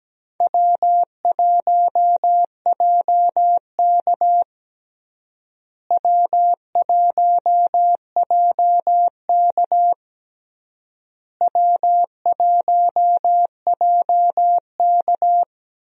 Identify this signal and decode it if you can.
Morse code, heard as a keyed tone, about 17 wpm, 700 Hz. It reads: W1JK W1JK W1JK
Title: USA Call Signs - 4 Characters - Repeated 3x - 17wpm